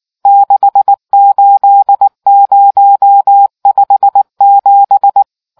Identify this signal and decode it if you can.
Morse code, heard as a keyed tone, about 19 wpm, 800 Hz. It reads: 68057